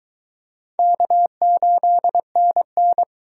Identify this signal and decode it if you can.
Morse code, heard as a keyed tone, about 23 wpm, 700 Hz. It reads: K8NN